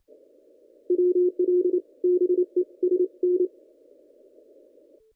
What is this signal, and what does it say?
Morse code, heard as a keyed tone, about 29 wpm, 360 Hz. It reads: WL BE S N
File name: contests / ss-cw-k5tr-op-n5rz -1800